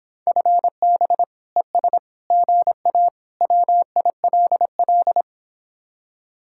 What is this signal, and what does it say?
Morse code, heard as a keyed tone, about 26 wpm, 700 Hz. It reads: FB ES GA WILL